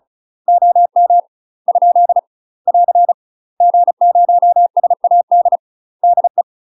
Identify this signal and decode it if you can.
Morse code, heard as a keyed tone, about 37 wpm, 700 Hz. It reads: OM ? <AR> G0SAD DE
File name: UK MASTER QSOs TRAINING / 35WPM / UK MASTER QSOs 0030